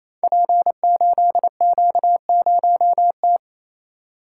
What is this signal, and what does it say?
Morse code, heard as a keyed tone, about 27 wpm, 700 Hz. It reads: P8Q0T
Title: Code Groups - 5 Letter:Numbers - 28wpm